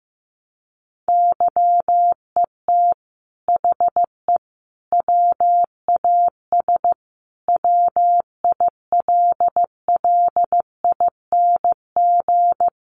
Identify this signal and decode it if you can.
Morse code, heard as a keyed tone, about 15 wpm, 700 Hz. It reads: YET HE WAS WILLING
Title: Sentences from Top 1100 Words 15wpm